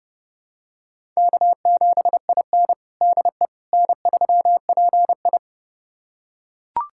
Morse code, heard as a keyed tone, about 32 wpm, 700 Hz. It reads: K7IN DE N3PI
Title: USA Callsign DE Callsign - 4 Characters - 30wpm